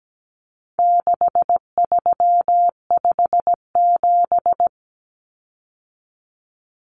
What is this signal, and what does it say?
Morse code, heard as a keyed tone, about 17 wpm, 700 Hz. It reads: 6357
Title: Common Cut Numbers - 4-Digit 17wpm